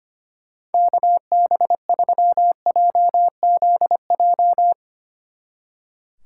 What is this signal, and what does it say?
Morse code, heard as a keyed tone, about 32 wpm, 700 Hz. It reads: KB3JZJ